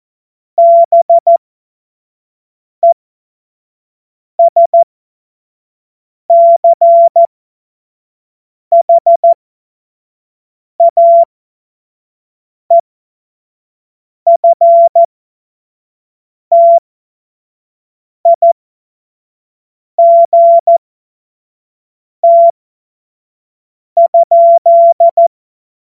Morse code, heard as a keyed tone, about 14 wpm, spaced as for 5 wpm, 680 Hz. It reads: BESCHAEFTIGT?
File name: Morsekurs / Witze / Witz 07. Dezember 2021, 25 F70